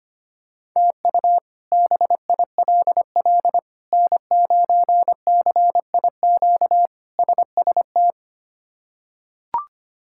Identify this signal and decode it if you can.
Morse code, heard as a keyed tone, about 25 wpm, 700 Hz. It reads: TU BILL N9CIQ SST
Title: K1USN SST QSO Practice - 25wpm